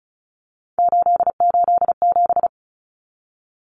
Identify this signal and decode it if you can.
Morse code, heard as a keyed tone, about 34 wpm, 700 Hz. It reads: 887
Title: Numbers - 3-Digit 35wpm